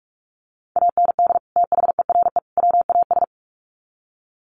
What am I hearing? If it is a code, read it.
Morse code, heard as a keyed tone, about 44 wpm, 700 Hz. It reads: AND THERE WAS
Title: Sets of 3 Words 45wpm